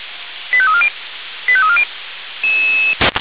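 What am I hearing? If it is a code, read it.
Five-tone selective call (ZVEI), heard as quick runs of five tones, tones of about 70 ms. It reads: 85340 85340
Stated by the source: ZVEI five-tone selective call 85320, received by SDR.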